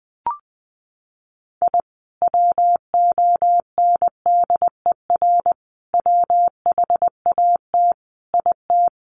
Morse code, heard as a keyed tone, about 20 wpm, 700 Hz. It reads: I WONDER WHAT IT
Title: Sentences from Top 600 Words 20wpm